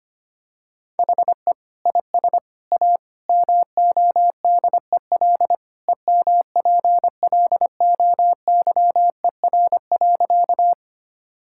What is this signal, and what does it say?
Morse code, heard as a keyed tone, about 25 wpm, 700 Hz. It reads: HE IS A MODEL EMPLOYER.
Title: Sentences from Top 1600 Words 25wpm